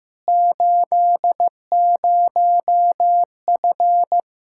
Morse code, heard as a keyed tone, about 15 wpm, 700 Hz. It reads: 80F
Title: K1USN SST QSO Practice 15wpm